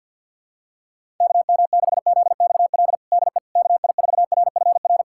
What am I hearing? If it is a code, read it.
Morse code, heard as a keyed tone, about 49 wpm, 700 Hz. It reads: KM6ZXL DE KI4RFR
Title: USA Callsign DE Callsign 50wpm